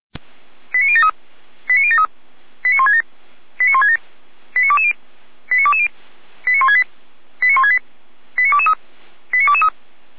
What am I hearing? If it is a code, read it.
Five-tone selective call (ZVEI), heard as quick runs of five tones, tones of about 70 ms. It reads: 89073 89073 89167 89167 89220 89220 89168 89168 89223 89223